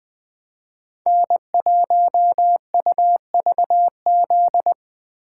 Morse code, heard as a keyed tone, about 20 wpm, 700 Hz. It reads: N1UVZ
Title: USA Call Signs - 5 Characters - 2x Character Spacing - 20wpm